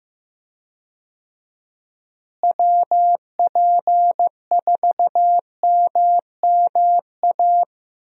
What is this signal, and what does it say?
Morse code, heard as a keyed tone, about 15 wpm, 700 Hz. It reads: WP4MMA
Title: USA Call Signs - 6 Characters - Repeated 2x - 15wpm